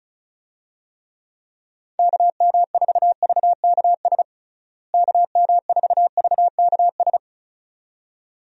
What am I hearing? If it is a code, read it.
Morse code, heard as a keyed tone, about 36 wpm, 700 Hz. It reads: KM4VKS KM4VKS
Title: USA Call Signs - 6 Characters - Repeated 2x - 35wpm